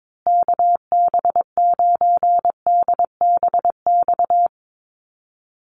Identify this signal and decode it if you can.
Morse code, heard as a keyed tone, about 22 wpm, 700 Hz. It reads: KB9DBX
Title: USA Call Signs Repeated 3x 22wpm